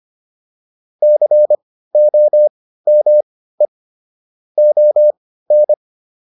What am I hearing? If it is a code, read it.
Morse code, heard as a keyed tone, about 25 wpm, spaced as for 15 wpm, 600 Hz. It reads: COME ON
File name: PR305 15